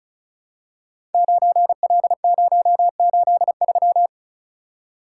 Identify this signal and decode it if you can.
Morse code, heard as a keyed tone, about 36 wpm, 700 Hz. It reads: 9L083
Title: Code Groups - 5 Letter:Numbers - 35wpm